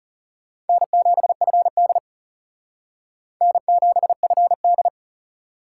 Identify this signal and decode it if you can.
Morse code, heard as a keyed tone, about 36 wpm, 700 Hz. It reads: N7FD N7FD
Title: Speed Racing - USA 1x2 Call Signs 30wpm